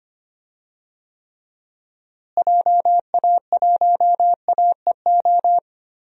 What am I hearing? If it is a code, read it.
Morse code, heard as a keyed tone, about 25 wpm, 700 Hz. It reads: JA1AEO
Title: DXCC-100 Call Signs - 6 Characters - Repeated 2x - 25wpm